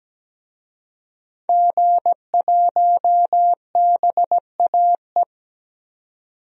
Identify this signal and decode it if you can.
Morse code, heard as a keyed tone, about 17 wpm, 700 Hz. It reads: G1BAE
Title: DXCC-100 Call Signs - 5 Characters - 17wpm